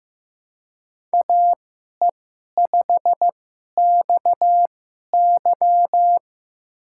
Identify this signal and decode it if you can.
Morse code, heard as a keyed tone, about 15 wpm, spaced as for 11 wpm, 700 Hz. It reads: AE5XY
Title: USA Call Signs - 5 Characters - 2x Character Spacing - 15wpm